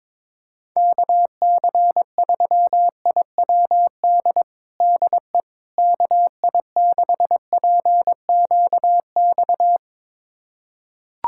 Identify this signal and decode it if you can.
Morse code, heard as a keyed tone, about 22 wpm, 700 Hz. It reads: KC3IWD DE KI6PQX
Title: USA Callsign DE Callsign - 6 Characters - 22wpm